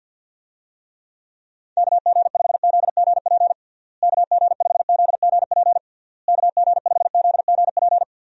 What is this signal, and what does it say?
Morse code, heard as a keyed tone, about 52 wpm, 700 Hz. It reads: KG5ZGP KG5ZGP KG5ZGP
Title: USA Call Signs - 6 Characters - Repeated 3x - 50wpm